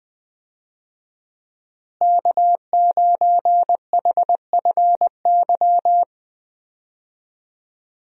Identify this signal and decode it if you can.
Morse code, heard as a keyed tone, about 20 wpm, 700 Hz. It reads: K9HFY
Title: USA Call Signs - 5 Characters - Repeated 2x - 20wpm